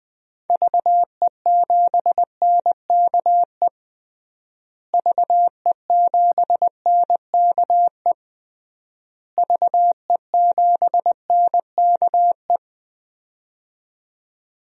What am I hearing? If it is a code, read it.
Morse code, heard as a keyed tone, about 20 wpm, 700 Hz. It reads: VE7NKE VE7NKE VE7NKE
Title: DXCC-100 Call Signs - 6 Characters - Repeated 3x - 20wpm